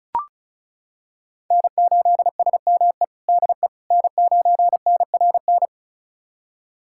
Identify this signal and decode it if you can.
Morse code, heard as a keyed tone, about 34 wpm, 700 Hz. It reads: N8SME DE N9NRN
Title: USA Callsign DE Callsign - 5 Characters - 35wpm